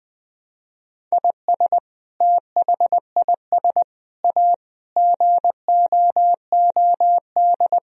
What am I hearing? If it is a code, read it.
Morse code, heard as a keyed tone, about 20 wpm, 700 Hz. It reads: IS THIS A GOOD
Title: Sentences from Top 1000 Words - Review Part 4 - 20wpm